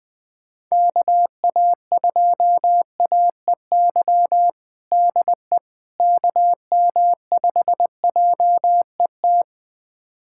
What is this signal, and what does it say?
Morse code, heard as a keyed tone, about 20 wpm, 700 Hz. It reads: KA2AEY DE KM5JET
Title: USA Callsign DE Callsign - 6 Characters - 20wpm